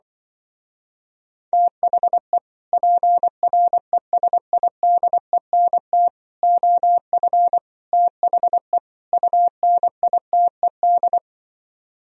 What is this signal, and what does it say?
Morse code, heard as a keyed tone, about 25 wpm, 700 Hz. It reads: THE PRESIDENT OF THE UNITED